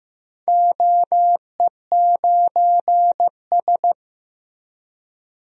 Morse code, heard as a keyed tone, about 15 wpm, 700 Hz. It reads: OE9S
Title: DXCC-100 Call Signs - 4 Characters - Repeated 2x - 15wpm